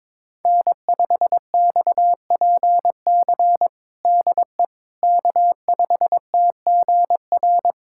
Morse code, heard as a keyed tone, about 22 wpm, 700 Hz. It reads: N5XPC DE K5TGR
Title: USA Callsign DE Callsign - 5 Characters - 22wpm